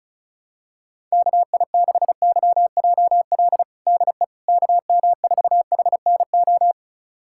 Morse code, heard as a keyed tone, about 36 wpm, 700 Hz. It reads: KI6YJL DE KM4HNO
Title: USA Callsign DE Callsign - 6 Characters - 35wpm